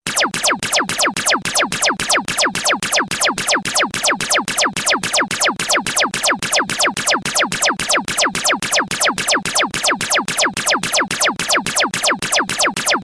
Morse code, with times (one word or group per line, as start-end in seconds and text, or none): none